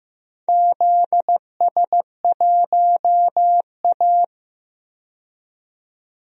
Morse code, heard as a keyed tone, about 15 wpm, 700 Hz. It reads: ZS1A
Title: DXCC-100 Call Signs - 4 Characters - 15wpm